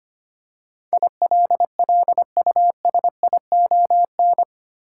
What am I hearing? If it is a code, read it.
Morse code, heard as a keyed tone, about 25 wpm, 700 Hz. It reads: ILLUSION
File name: 8 Letter Words - First 3 Letters Given - 25wpm